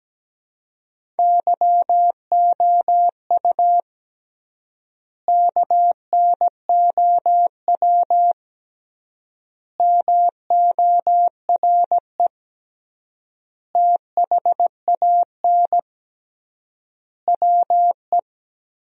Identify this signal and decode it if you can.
Morse code, heard as a keyed tone, about 17 wpm, 700 Hz. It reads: YOU KNOW MORE THAN WE
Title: 3x Word-Spacing - Sentences from Top 100 Words - 17wpm